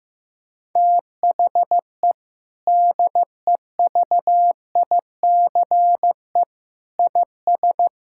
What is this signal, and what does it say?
Morse code, heard as a keyed tone, about 15 wpm, 700 Hz. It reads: THE DEVICE IS
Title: Sentences from Top 1600 Words 15wpm